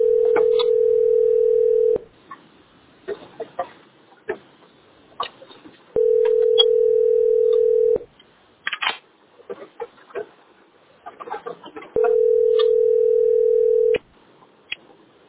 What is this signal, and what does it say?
Ringback tone (North America), 440 + 480 Hz.